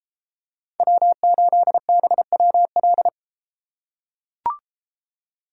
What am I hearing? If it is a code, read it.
Morse code, heard as a keyed tone, about 30 wpm, 700 Hz. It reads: W8BWL